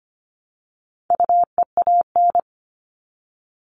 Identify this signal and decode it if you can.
Morse code, heard as a keyed tone, about 25 wpm, 700 Hz. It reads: UEAN
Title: All Cut Numbers - 4-Digit 25wpm